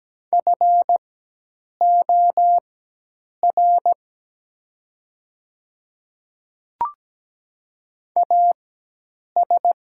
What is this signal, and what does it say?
Morse code, heard as a keyed tone, about 17 wpm, spaced as for 8 wpm, 700 Hz. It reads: FOR AS
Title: 4x Character Spacing - Top 200 Words - 17wpm